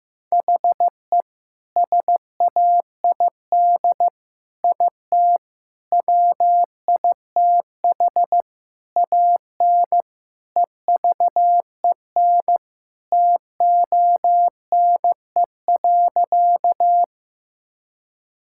Morse code, heard as a keyed tone, about 15 wpm, 700 Hz. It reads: HE SAID IT WITH AN EVEN TONE.
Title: Sentences from Top 1600 Words 15wpm